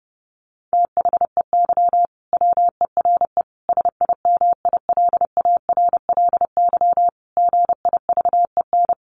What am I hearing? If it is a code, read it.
Morse code, heard as a keyed tone, about 30 wpm, 700 Hz. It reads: THEY WERE SIMILARLY GIVEN
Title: Sentences from Top 1400 Words 30wpm